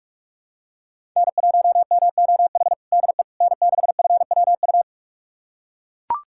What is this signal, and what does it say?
Morse code, heard as a keyed tone, about 42 wpm, 700 Hz. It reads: N1MOH DE N6FWU
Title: USA Callsign DE Callsign - 5 Characters - 45wpm